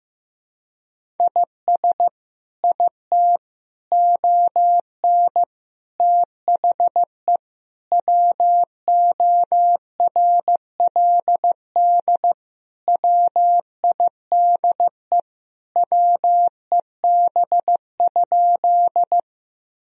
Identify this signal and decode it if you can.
Morse code, heard as a keyed tone, about 15 wpm, 700 Hz. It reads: IS IT ON THE WORLD WIDE WEB?
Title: Sentences from Top 1000 Words - Review Part 4 - 15wpm